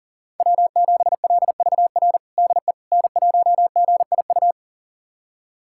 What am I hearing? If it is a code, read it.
Morse code, heard as a keyed tone, about 39 wpm, 690 Hz. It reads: W7LVR DE N1GIU
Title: USA Callsign DE Callsign - 5 Characters - 40wpm